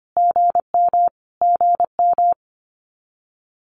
Morse code, heard as a keyed tone, about 24 wpm, 700 Hz. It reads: GM GM
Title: Abbreviations Repeated 3x - 25wpm